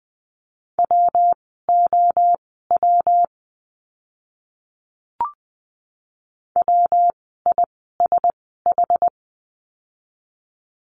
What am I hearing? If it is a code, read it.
Morse code, heard as a keyed tone, about 20 wpm, spaced as for 14 wpm, 700 Hz. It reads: WOW WISH
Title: 115 Lesson 19f - Letter w - Words 20wpm